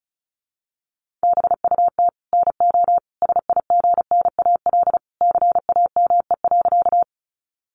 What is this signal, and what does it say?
Morse code, heard as a keyed tone, about 34 wpm, 700 Hz. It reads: BUT NO SIGNAL CAME.
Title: Sentences from Top 1400 Words 35wpm